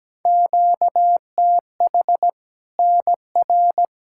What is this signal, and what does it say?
Morse code, heard as a keyed tone, about 17 wpm, 700 Hz. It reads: QTH NR
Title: Practice QSO - Exchange 1 - Beginning - 17wpm